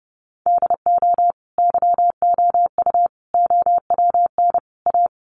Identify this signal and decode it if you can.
Morse code, heard as a keyed tone, about 30 wpm, 700 Hz. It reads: DO YOU OWN A
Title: Sentences from Top 400 Words 30wpm